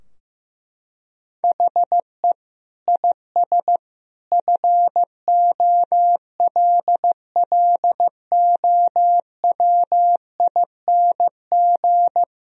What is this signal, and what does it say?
Morse code, heard as a keyed tone, about 15 wpm, 700 Hz. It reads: HE IS FOLLOWING
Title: Sentences from Top 1000 Words - Review Part 4 - 15wpm